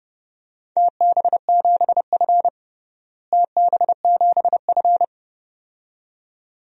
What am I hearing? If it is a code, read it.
Morse code, heard as a keyed tone, about 30 wpm, 700 Hz. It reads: TB7F TB7F
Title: DXCC-100 Call Signs - 4 Characters - Repeated 2x - 30wpm